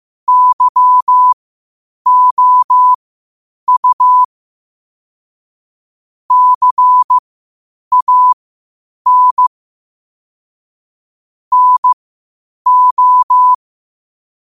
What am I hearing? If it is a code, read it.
Morse code, heard as a keyed tone, about 15 wpm, spaced as for 8 wpm, 1000 Hz. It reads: YOU CAN NO
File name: qotdfiles / 8wpm